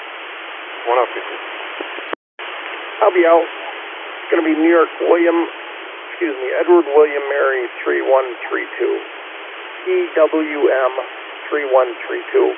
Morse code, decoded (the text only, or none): none